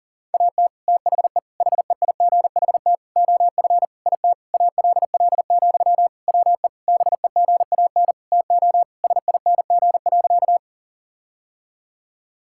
Morse code, heard as a keyed tone, about 40 wpm, 700 Hz. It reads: AT THE HEIGHT OF IT ALL, WE BEGAN TO SING.
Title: Sentences from Top 1600 Words 40wpm